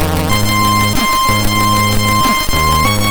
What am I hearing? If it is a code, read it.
There is no code.